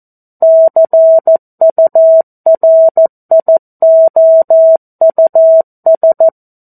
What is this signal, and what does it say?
Morse code, heard as a keyed tone, about 14 wpm, 650 Hz. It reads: CURIOUS
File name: 02-Chapter-02 14 WPM